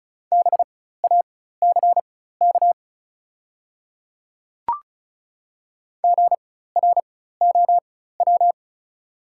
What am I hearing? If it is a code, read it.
Morse code, heard as a keyed tone, about 36 wpm, spaced as for 16 wpm, 700 Hz. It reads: BACK GROW
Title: 4x Character Spacing - Top 300 Words - 35wpm